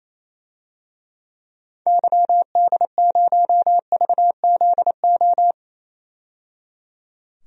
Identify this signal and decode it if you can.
Morse code, heard as a keyed tone, about 28 wpm, 700 Hz. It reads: YD0VZO
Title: DXCC-100 Call Signs - 6 Characters - 28wpm